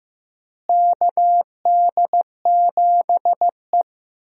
Morse code, heard as a keyed tone, about 15 wpm, 700 Hz. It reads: KD7E
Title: USA Call Signs - 4 Characters - Repeated 2x - 15wpm